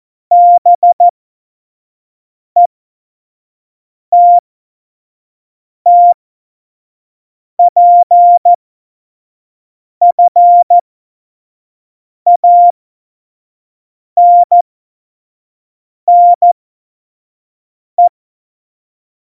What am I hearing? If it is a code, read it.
Morse code, heard as a keyed tone, about 14 wpm, spaced as for 5 wpm, 700 Hz. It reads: BETTPFANNE
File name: Morsekurs / Witze / Witz 08. September 2020, 25 F70